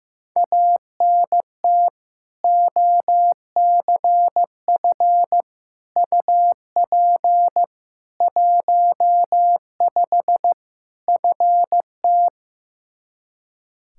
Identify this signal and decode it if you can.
Morse code, heard as a keyed tone, about 15 wpm, 700 Hz. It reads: ANT OCF UP 15 FT
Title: Practice QSO - Exchange 2 - Info - 15wpm